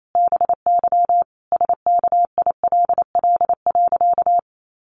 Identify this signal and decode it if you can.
Morse code, heard as a keyed tone, about 28 wpm, 700 Hz. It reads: BY SKILL.
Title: Sentences from Top 1200 Words 28wpm